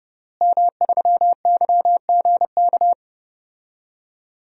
Morse code, heard as a keyed tone, about 32 wpm, 700 Hz. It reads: M3YGK
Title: DXCC-100 Call Signs - 5 Characters - Repeated 2x - 30wpm